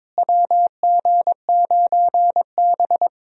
Morse code, heard as a keyed tone, about 22 wpm, 700 Hz. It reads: WG9B
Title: USA Call Signs - 4 Characters - Repeated 2x - 22wpm